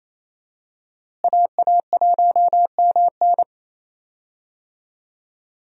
Morse code, heard as a keyed tone, about 28 wpm, 700 Hz. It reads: AA1MN
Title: USA Call Signs - 5 Characters - Repeated 3x - 28wpm